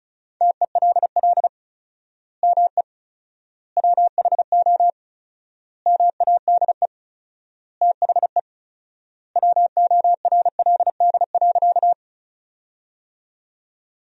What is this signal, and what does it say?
Morse code, heard as a keyed tone, about 34 wpm, 700 Hz. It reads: TELL ME WHO MADE THE WORLD.
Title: 4x Word-Spacing - Sentences from Top 200 Words - 35wpm